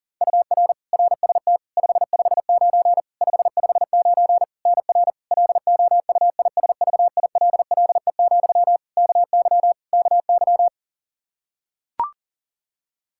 Morse code, heard as a keyed tone, about 40 wpm, 700 Hz. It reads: UR RST 559 559 NR LOUISVILLE, KY KY
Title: Practice QSO - Exchange 1 - Beginning - 40wpm